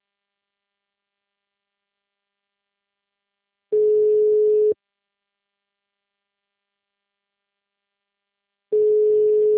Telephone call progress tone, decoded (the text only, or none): ringback tone